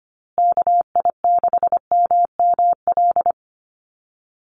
Morse code, heard as a keyed tone, about 25 wpm, 700 Hz. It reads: KI6MML
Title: USA Call Signs Repeated 3x 25wpm